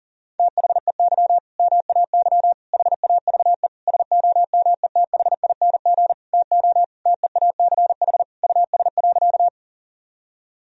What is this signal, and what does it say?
Morse code, heard as a keyed tone, about 40 wpm, 700 Hz. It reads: THEY MAY HAVE SOMETHING TO TEACH US.
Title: Sentences from Top 1000 Words - Review Part 2 - 40wpm